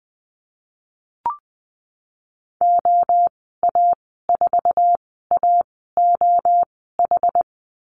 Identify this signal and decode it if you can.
Morse code, heard as a keyed tone, about 20 wpm, spaced as for 15 wpm, 700 Hz. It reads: OA4AOH
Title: 076 Lesson 13g - Letter L - Call Signs 20wpm